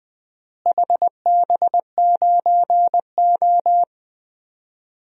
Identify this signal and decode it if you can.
Morse code, heard as a keyed tone, about 20 wpm, 700 Hz. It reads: HB9O
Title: DXCC-100 Call Signs - 4 Characters - Repeated 3x - 20wpm